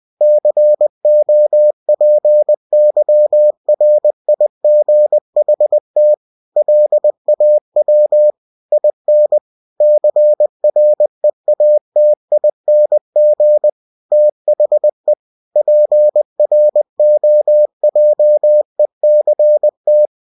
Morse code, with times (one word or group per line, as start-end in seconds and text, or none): 0.2-6.1 COPYRIGHT
6.6-8.3 LAW
8.7-9.4 IN
9.8-13.7 CREATING
14.1-15.1 THE
15.6-20.1 PROJECT